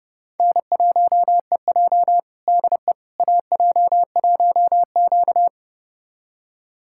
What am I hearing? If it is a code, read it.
Morse code, heard as a keyed tone, about 32 wpm, 700 Hz. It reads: N1EJ DE AJ1Q
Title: USA Callsign DE Callsign - 4 Characters - 30wpm